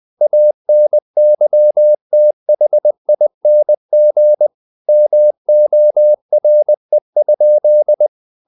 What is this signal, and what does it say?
Morse code, heard as a keyed tone, about 20 wpm, 600 Hz. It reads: ANYTHING MORE?
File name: the old man and the sea part 2 20wpm